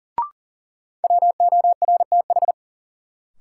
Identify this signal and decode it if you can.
Morse code, heard as a keyed tone, about 40 wpm, 700 Hz. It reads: WORTH
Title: Sentences from Top 800 Words - New Word Drill - 40wpm